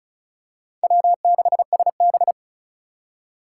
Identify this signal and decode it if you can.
Morse code, heard as a keyed tone, about 36 wpm, 700 Hz. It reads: W6SB
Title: USA Call Signs - 3x Character Spacing - 35wpm